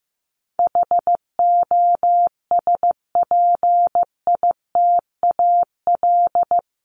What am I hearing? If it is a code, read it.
Morse code, heard as a keyed tone, about 15 wpm, 700 Hz. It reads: HOSPITAL
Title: Top 1751-2000 Words 15wpm